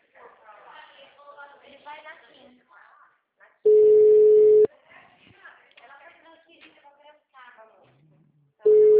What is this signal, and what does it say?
Ringback tone (Europe, 425 Hz), with speech.
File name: out- -2015- - - .686